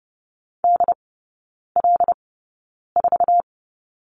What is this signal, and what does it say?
Morse code, heard as a keyed tone, about 30 wpm, spaced as for 9 wpm, 700 Hz. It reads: DL4
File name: Lesson 15e - Number 5 - Three Characters 30wpm